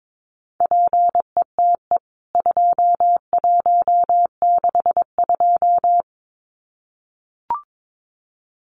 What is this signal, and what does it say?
Morse code, heard as a keyed tone, about 22 wpm, 700 Hz. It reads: PETE 2162
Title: CWT Exchange 22wpm